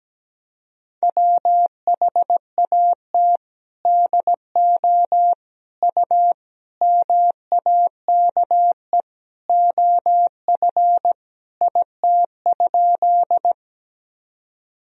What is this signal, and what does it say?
Morse code, heard as a keyed tone, about 17 wpm, 700 Hz. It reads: WHAT DO U MAKE OF IT?
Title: Sentences from Top 100 Words and Abbreviations 17wpm